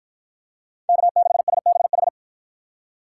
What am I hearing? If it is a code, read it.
Morse code, heard as a keyed tone, about 53 wpm, 700 Hz. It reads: K6SBH